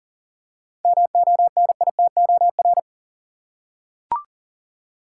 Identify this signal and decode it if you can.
Morse code, heard as a keyed tone, about 38 wpm, 700 Hz. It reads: MONITOR
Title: Seven Letter Words - 40wpm